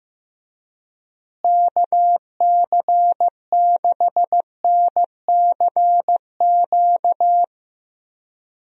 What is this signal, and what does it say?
Morse code, heard as a keyed tone, about 15 wpm, 700 Hz. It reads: KC6NCQ
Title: USA Call Signs - 6 Characters - Repeated 2x - 15wpm